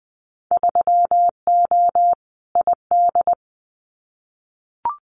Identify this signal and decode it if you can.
Morse code, heard as a keyed tone, about 20 wpm, 700 Hz. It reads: 3O ID
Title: Winter Field Day Exchanges Encore 20wpm